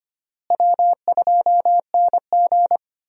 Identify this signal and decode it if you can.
Morse code, heard as a keyed tone, about 25 wpm, 700 Hz. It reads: W2NG
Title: USA Call Signs - 4 Characters - Repeated 3x - 25wpm